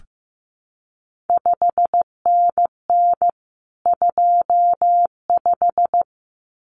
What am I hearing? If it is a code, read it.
Morse code, heard as a keyed tone, about 15 wpm, 700 Hz. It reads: 5NN 25
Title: CQ WW DX Contest Exchange - 15wpm